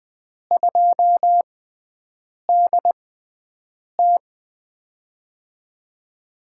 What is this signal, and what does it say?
Morse code, heard as a keyed tone, about 20 wpm, spaced as for 7 wpm, 700 Hz. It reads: 2DT
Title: USA Call Signs - 4 Characters - 6x Character Spacing - 20wpm